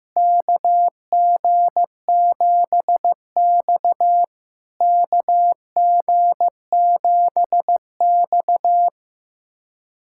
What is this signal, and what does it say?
Morse code, heard as a keyed tone, about 15 wpm, 700 Hz. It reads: KG7X KG7X
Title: SOTA QSO Practice 15wpm